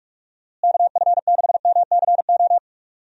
Morse code, heard as a keyed tone, about 43 wpm, 700 Hz. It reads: KF6MCO
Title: USA Call Signs - 6 Characters - Repeated 3x - 45wpm